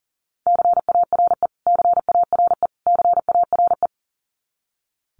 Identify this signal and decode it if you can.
Morse code, heard as a keyed tone, about 37 wpm, 700 Hz. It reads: CARE CARE CARE
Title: Top 500 Words Repeated 40wpm